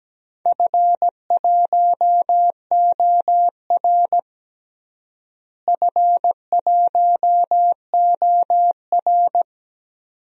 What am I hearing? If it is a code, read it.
Morse code, heard as a keyed tone, about 17 wpm, 700 Hz. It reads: F1OR F1OR
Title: DXCC-100 Call Signs - 4 Characters - Repeated 2x - 17wpm